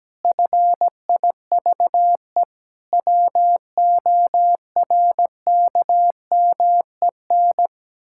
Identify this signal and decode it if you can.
Morse code, heard as a keyed tone, about 17 wpm, 700 Hz. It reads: FIVE WORKMEN
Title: Pangram Challenge v2 - 17wpm